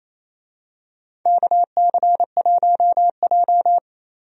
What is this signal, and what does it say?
Morse code, heard as a keyed tone, about 28 wpm, 700 Hz. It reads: KC1J
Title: USA Call Signs - 4 Characters - 28wpm